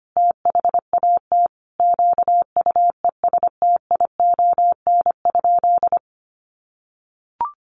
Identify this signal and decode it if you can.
Morse code, heard as a keyed tone, about 25 wpm, 700 Hz. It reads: THAT QUESTION?
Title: Sentences from Top 400 Words 25wpm